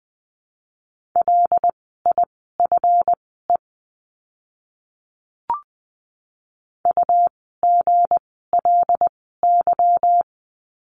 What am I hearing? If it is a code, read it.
Morse code, heard as a keyed tone, about 20 wpm, spaced as for 14 wpm, 700 Hz. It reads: LIFE UGLY
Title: 160 Lesson 26f - Letter g - Words 20wpm